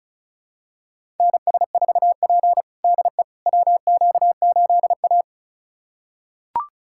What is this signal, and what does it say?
Morse code, heard as a keyed tone, about 34 wpm, 700 Hz. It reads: NS4P DE WQ8A
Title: USA Callsign DE Callsign - 4 Characters - 35wpm